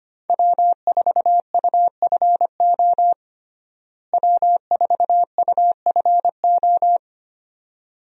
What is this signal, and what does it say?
Morse code, heard as a keyed tone, about 25 wpm, 700 Hz. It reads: W4UFO W4UFO
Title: USA Call Signs - 5 Characters - Repeated 2x - 25wpm